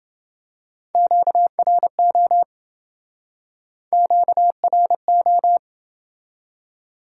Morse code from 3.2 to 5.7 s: QRO

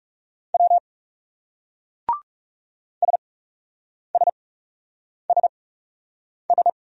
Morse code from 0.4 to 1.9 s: W

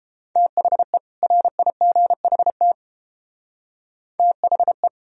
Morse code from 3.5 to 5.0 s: THE

